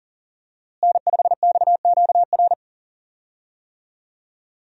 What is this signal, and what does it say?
Morse code, heard as a keyed tone, about 38 wpm, 700 Hz. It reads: N5XQR
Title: USA Call Signs - 5 Characters - Repeated 2x - 40wpm